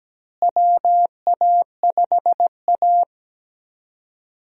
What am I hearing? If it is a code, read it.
Morse code, heard as a keyed tone, about 17 wpm, 700 Hz. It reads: WA5A
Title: USA Call Signs - 4 Characters - Repeated 2x - 17wpm